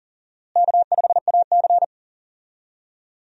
Morse code, heard as a keyed tone, about 40 wpm, 700 Hz. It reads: K5AC
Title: USA Call Signs - 4 Characters - Repeated 2x - 40wpm